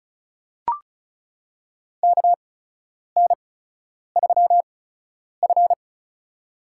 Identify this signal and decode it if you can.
Morse code, heard as a keyed tone, about 34 wpm, spaced as for 10 wpm, 700 Hz. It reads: KN3F
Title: USA Call Signs - 4 Characters - 8x Character Spacing - 35wpm